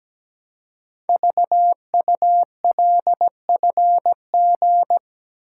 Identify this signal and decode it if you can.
Morse code, heard as a keyed tone, about 17 wpm, 700 Hz. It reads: VULFG
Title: Code Groups - 5 Letters - 17wpm